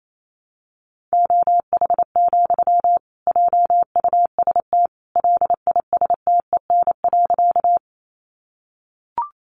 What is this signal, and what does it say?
Morse code, heard as a keyed tone, about 28 wpm, 700 Hz. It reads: OH, JUST LISTEN.
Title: Sentences from Top 500 Words 28wpm